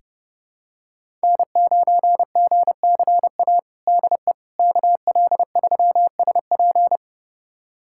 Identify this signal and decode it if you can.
Morse code, heard as a keyed tone, about 30 wpm, 700 Hz. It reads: N9GCA DE KL3SP
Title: USA Callsign DE Callsign - 5 Characters - 30wpm